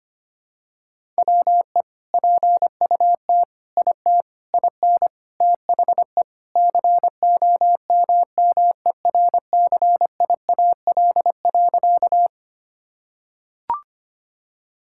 Morse code, heard as a keyed tone, about 25 wpm, 700 Hz. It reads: WE PUT IT IN THE COMMERCIAL.
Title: Sentences from Top 1000 Words - Review Part 5 - 25wpm